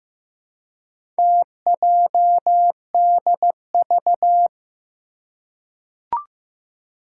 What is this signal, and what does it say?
Morse code, heard as a keyed tone, about 15 wpm, 700 Hz. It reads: TJDV
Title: Four Challenging Letter-Number Combinations 15wpm